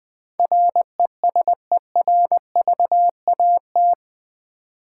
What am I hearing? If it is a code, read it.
Morse code, heard as a keyed tone, about 20 wpm, 700 Hz. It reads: RESERVAT
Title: Build A Word - 4 Letters At A Time - 11-Letter Words - 20wpm